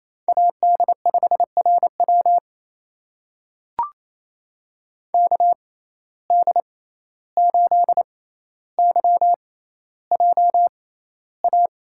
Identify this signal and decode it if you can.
Morse code, heard as a keyed tone, about 28 wpm, spaced as for 14 wpm, 700 Hz. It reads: AD5RW KD8YJA
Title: USA Call Signs - 6x Character Spacing - 28wpm